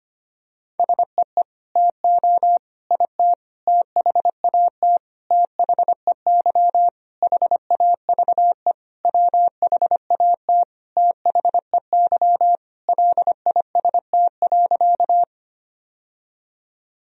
Morse code, heard as a keyed tone, about 25 wpm, 700 Hz. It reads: SEE TO IT THAT THEY HAVE WHAT THEY LIST.
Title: Sentences from Top 1000 Words - Review Part 2 - 25wpm